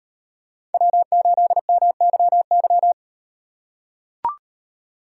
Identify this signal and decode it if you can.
Morse code, heard as a keyed tone, about 38 wpm, 700 Hz. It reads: W8MYY